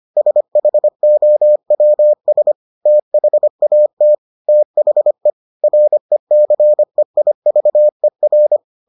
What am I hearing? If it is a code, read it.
Morse code, heard as a keyed tone, about 25 wpm, 600 Hz. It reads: SHOWS THAT THE RECEIVER